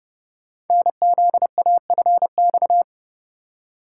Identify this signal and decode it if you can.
Morse code, heard as a keyed tone, about 30 wpm, 700 Hz. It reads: NZAFX
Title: Code Groups - 5 Letters - 30wpm